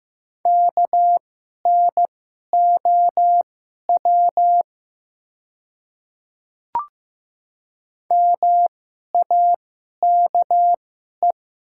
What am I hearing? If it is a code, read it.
Morse code, heard as a keyed tone, about 15 wpm, spaced as for 11 wpm, 700 Hz. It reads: KNOW MAKE
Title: 2x Character Spacing - Top 100 Words - 15wpm